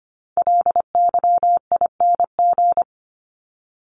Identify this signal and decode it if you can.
Morse code, heard as a keyed tone, about 25 wpm, 700 Hz. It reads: LYING
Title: ING Suffix - Top 100 Words with Base Words 25wpm